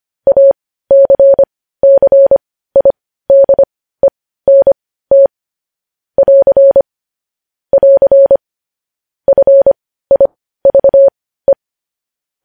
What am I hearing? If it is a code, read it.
Morse code, heard as a keyed tone, about 25 wpm, spaced as for 15 wpm, 550 Hz. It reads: ACCIDENT <AR> <AR> FIVE
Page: News Headlines in Morse Code at 15 WPM